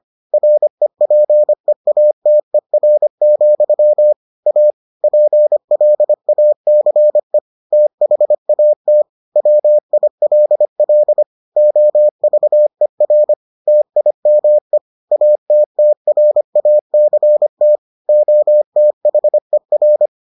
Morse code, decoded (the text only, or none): REPEATER, A PLACE THAT WILL OVER TIME ATTRACT OTHER